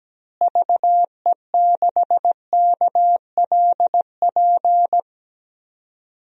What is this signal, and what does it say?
Morse code, heard as a keyed tone, about 17 wpm, 700 Hz. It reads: VE6KLP
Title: DXCC-100 Call Signs - 6 Characters - Repeated 2x - 17wpm